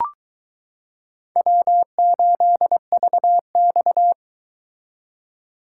Morse code, heard as a keyed tone, about 23 wpm, 700 Hz. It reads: W8VX